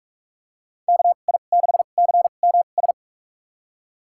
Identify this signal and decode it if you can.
Morse code, heard as a keyed tone, about 46 wpm, spaced as for 32 wpm, 700 Hz. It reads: KI6CMS
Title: USA Call Signs - 2x Character Spacing - 45wpm